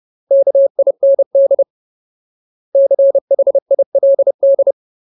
Morse code, heard as a keyed tone, about 30 wpm, 550 Hz. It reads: KIND CHILD
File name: Ultimate Word List / 30WPM Ultimate